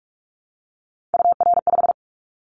Morse code, heard as a keyed tone, about 46 wpm, 700 Hz. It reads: UR5
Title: Callsign Prefixes for Top 25 DXCC - 45wpm